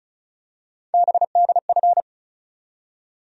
Morse code, heard as a keyed tone, about 36 wpm, 700 Hz. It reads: BDF
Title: USA Call Sign Postfixes - 3 Letters 35wpm